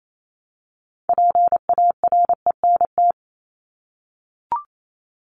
Morse code, heard as a keyed tone, about 28 wpm, 700 Hz. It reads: PARENT
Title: Top 1751-2000 Words 28wpm